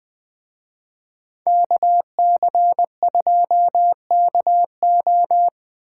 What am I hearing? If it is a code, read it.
Morse code, heard as a keyed tone, about 20 wpm, 700 Hz. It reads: KC2KO
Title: Speed Racing - USA 2x2 Call Signs 17wpm